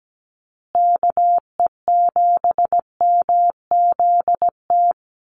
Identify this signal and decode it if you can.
Morse code, heard as a keyed tone, about 17 wpm, 700 Hz. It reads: KE7MZT
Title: USA Call Signs Repeated 2x 17wpm